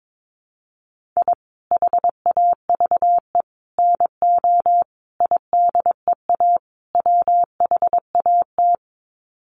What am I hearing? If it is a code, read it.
Morse code, heard as a keyed tone, about 22 wpm, 700 Hz. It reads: I HAVE NO IDEA WHAT
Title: Sets of 5 Words Encore 22wpm